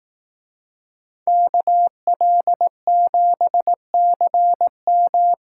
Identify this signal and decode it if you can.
Morse code, heard as a keyed tone, about 18 wpm, 700 Hz. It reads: KL7CM
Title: Speed Racing - USA 2x2 Call Signs 15wpm